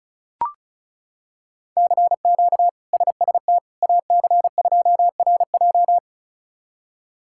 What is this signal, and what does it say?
Morse code, heard as a keyed tone, about 36 wpm, 700 Hz. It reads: CQ SST AC2RJ
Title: K1USN SST QSO Practice - 35wpm